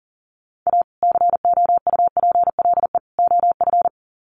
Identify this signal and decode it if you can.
Morse code, heard as a keyed tone, about 40 wpm, 700 Hz. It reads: A COUPLE OF
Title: Sets of 3 Words 40wpm